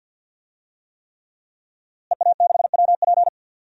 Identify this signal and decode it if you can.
Morse code, heard as a keyed tone, about 52 wpm, 700 Hz. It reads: EA6WP
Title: DXCC-100 Call Signs - 5 Characters - Repeated 3x - 50wpm